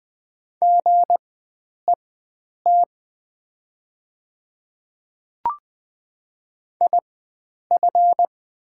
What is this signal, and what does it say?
Morse code, heard as a keyed tone, about 20 wpm, spaced as for 9 wpm, 700 Hz. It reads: GET IF
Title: 4x Character Spacing - Top 500 Words - 20wpm